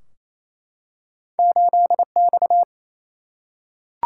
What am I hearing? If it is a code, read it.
Morse code, heard as a keyed tone, about 28 wpm, 700 Hz. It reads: 8X
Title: Single-Letter Single-Number 28wpm